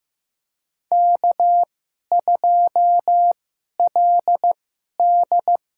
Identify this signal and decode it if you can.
Morse code, heard as a keyed tone, about 15 wpm, spaced as for 11 wpm, 700 Hz. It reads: K2LD
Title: USA Call Signs - 4 Characters - 2x Character Spacing - 15wpm